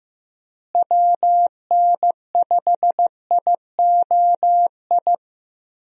Morse code, heard as a keyed tone, about 15 wpm, 700 Hz. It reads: WN5IOI
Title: USA Call Signs - 6 Characters - Repeated 2x - 15wpm